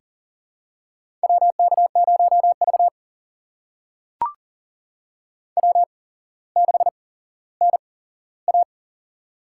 Morse code, heard as a keyed tone, about 42 wpm, spaced as for 17 wpm, 700 Hz. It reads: WK0V W6NA
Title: USA Call Signs - 4 Characters - 8x Character Spacing - 40wpm